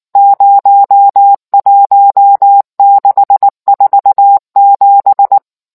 Morse code, heard as a keyed tone, about 19 wpm, 800 Hz. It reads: 01647